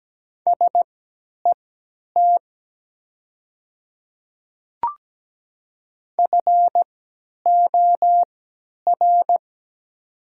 Morse code, heard as a keyed tone, about 17 wpm, spaced as for 10 wpm, 700 Hz. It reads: SET FOR
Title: 3x Character Spacing - Top 200 Words - 17wpm